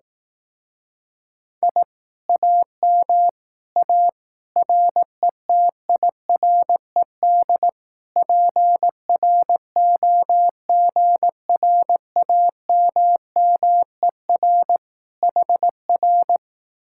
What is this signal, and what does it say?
Morse code, heard as a keyed tone, about 18 wpm, 700 Hz. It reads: I AM A RETIRED PROGRAMMER HR
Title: Speed-Racing Common QSO Phrases Encore - 15wpm